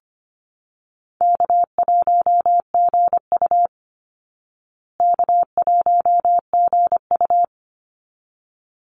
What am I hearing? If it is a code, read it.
Morse code, heard as a keyed tone, about 25 wpm, 700 Hz. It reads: K1GU K1GU
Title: Easy Call Signs Repeated 2x - 25wpm